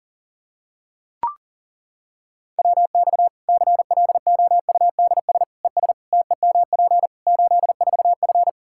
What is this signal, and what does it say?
Morse code, heard as a keyed tone, about 40 wpm, 700 Hz. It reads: WX CLOUDS ES TEMP 84F
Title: Practice QSO - Exchange 2 - Info - 40wpm